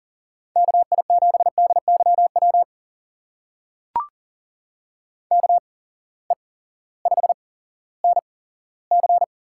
Morse code, heard as a keyed tone, about 39 wpm, spaced as for 18 wpm, 700 Hz. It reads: KI7DYW KE5NC
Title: USA Call Signs - 6 Characters - 8x Character Spacing - 40wpm